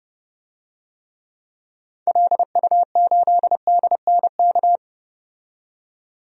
Morse code, heard as a keyed tone, about 31 wpm, 700 Hz. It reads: LU8DNK
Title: DXCC-100 Call Signs - 6 Characters - 30wpm